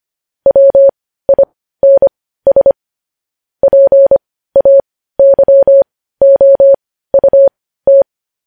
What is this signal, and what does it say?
Morse code, heard as a keyed tone, about 25 wpm, spaced as for 15 wpm, 550 Hz. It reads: WINS PAYOUT
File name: podcast / News headlines-1x-25x15